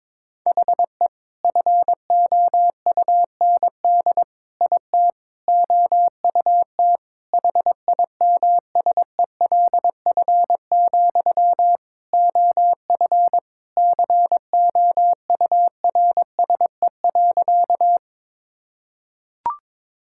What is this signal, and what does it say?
Morse code, heard as a keyed tone, about 22 wpm, 700 Hz. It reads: HE FOUND IT OUT HIMSELF, OF COURSE.
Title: Sentences from Top 1600 Words 22wpm